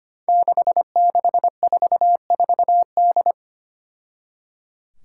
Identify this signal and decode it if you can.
Morse code, heard as a keyed tone, about 25 wpm, 700 Hz. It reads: 6644D
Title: Five Challenging Letter-Number Combinations 25wpm